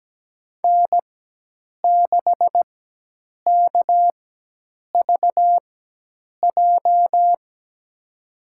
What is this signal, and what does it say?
Morse code, heard as a keyed tone, about 17 wpm, spaced as for 8 wpm, 700 Hz. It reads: N6KVJ
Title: USA Call Signs - 5 Characters - 4x Character Spacing - 17wpm